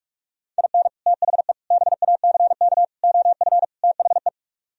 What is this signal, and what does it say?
Morse code, heard as a keyed tone, about 46 wpm, 700 Hz. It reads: IN THE BACK OF THE